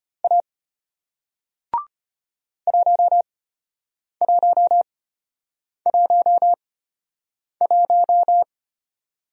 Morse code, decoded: A 1111

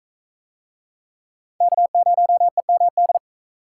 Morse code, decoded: K0EMD

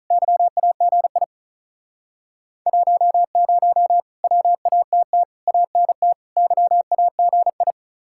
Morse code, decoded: YAGI 10 WATT ANT YAGI

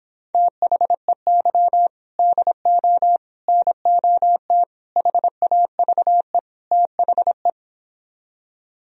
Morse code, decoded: THEY DO NOT HAVE THE

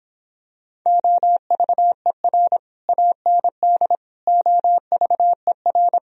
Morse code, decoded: OVER AND OVER

